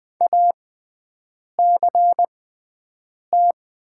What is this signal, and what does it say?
Morse code, heard as a keyed tone, about 20 wpm, spaced as for 7 wpm, 700 Hz. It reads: ACT
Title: 6x Character Spacing - Top 200 Words - 20wpm